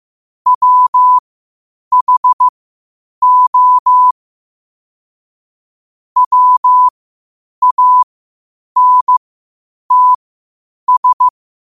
Morse code, heard as a keyed tone, about 15 wpm, spaced as for 8 wpm, 1000 Hz. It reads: WHO WANTS
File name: 8wpm